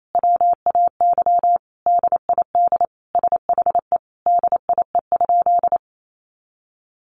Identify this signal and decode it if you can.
Morse code, heard as a keyed tone, about 28 wpm, 700 Hz. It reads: WAY DID SHE DIE?
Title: Sentences from Top 1300 Words 28wpm